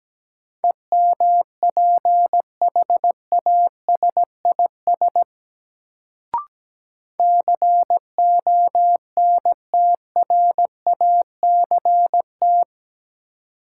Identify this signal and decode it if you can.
Morse code, heard as a keyed tone, about 17 wpm, 700 Hz. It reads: EMPHASIS CONTRACT
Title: Sentences from Top 1400 Words - New Word Drill - 17wpm